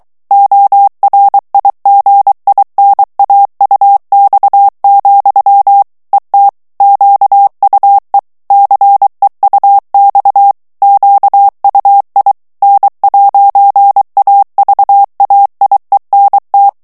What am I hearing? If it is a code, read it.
Morse code, heard as a keyed tone, about 23 wpm, 800 Hz. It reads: ORIGINAUX, ET QUE CEUX QUI N'AVAIENT
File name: CW / FICHIERS-MP3 / FRANCAIS TEXTE CLAIR 2 VITESSE 22 MOTS MINUTES